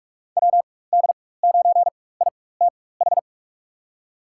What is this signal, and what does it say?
Morse code, heard as a keyed tone, about 45 wpm, spaced as for 21 wpm, 700 Hz. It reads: WD9ITH
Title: USA Call Signs - 6 Characters - 4x Character Spacing - 45wpm